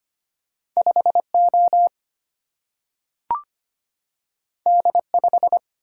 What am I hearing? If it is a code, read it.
Morse code, heard as a keyed tone, about 25 wpm, 700 Hz. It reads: 5 O D 5
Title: Single-Letter Single-Number 25wpm